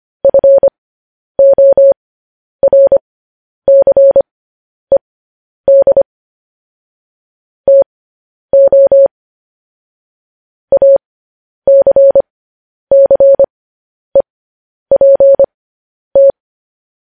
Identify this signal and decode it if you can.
Morse code, heard as a keyed tone, about 25 wpm, spaced as for 10 wpm, 550 Hz. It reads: FORCED TO ACCEPT
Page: news headlines in morse code at 10 wpm